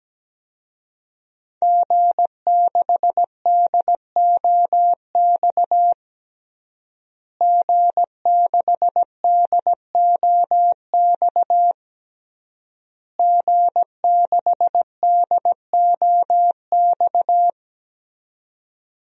Morse code, heard as a keyed tone, about 17 wpm, 700 Hz. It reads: G6DOX G6DOX G6DOX
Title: DXCC-100 Call Signs - 5 Characters - Repeated 3x - 17wpm